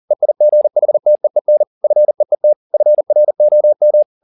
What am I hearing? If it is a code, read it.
Morse code, heard as a keyed tone, about 40 wpm, 600 Hz. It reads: EIGHTEEN FEET FROM
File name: the old man and the sea 2 40wpm